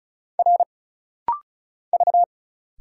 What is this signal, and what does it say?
Morse code, heard as a keyed tone, about 34 wpm, 700 Hz. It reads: R V
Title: Single Letter-Number Alt - Rapid-Fire 35wpm